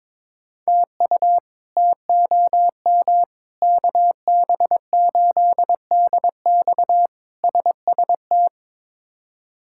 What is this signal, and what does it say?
Morse code, heard as a keyed tone, about 22 wpm, 700 Hz. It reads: TU TOM KB8DX SST
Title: K1USN SST QSO Practice - 22wpm